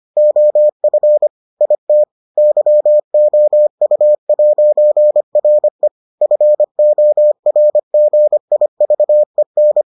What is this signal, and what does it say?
Morse code, heard as a keyed tone, about 25 wpm, 600 Hz. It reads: OF IT YOU'RE FORGIVEN